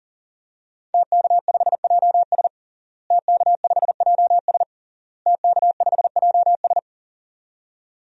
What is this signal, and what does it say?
Morse code, heard as a keyed tone, about 40 wpm, 700 Hz. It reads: TK5JS TK5JS TK5JS
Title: DXCC-100 Call Signs - 5 Characters - Repeated 3x - 40wpm